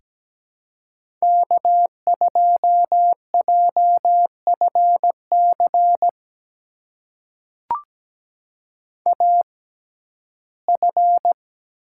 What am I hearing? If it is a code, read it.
Morse code, heard as a keyed tone, about 17 wpm, spaced as for 12 wpm, 700 Hz. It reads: K2JFC AF
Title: USA Call Signs - 5 Characters - 6x Character Spacing - 17wpm